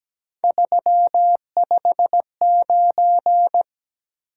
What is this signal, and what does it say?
Morse code, heard as a keyed tone, about 17 wpm, 700 Hz. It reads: 359
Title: Top 180 QSO Elements 17wpm